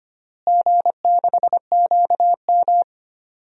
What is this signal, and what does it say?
Morse code, heard as a keyed tone, about 25 wpm, 700 Hz. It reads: G6QM
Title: DXCC-100 Call Signs - 4 Characters - Repeated 2x - 25wpm